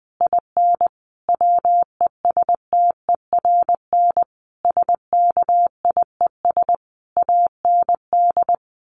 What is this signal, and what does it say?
Morse code, heard as a keyed tone, about 20 wpm, 700 Hz. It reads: IN WESTERN SKIES AND
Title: A Sound Of Thunder 20wpm